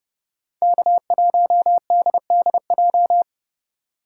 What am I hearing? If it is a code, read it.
Morse code, heard as a keyed tone, about 29 wpm, 700 Hz. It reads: K1DDJ